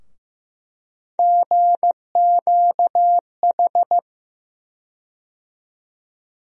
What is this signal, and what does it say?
Morse code, heard as a keyed tone, about 15 wpm, 700 Hz. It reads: GQH
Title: USA Call Sign Postfixes - 3 Letters 15wpm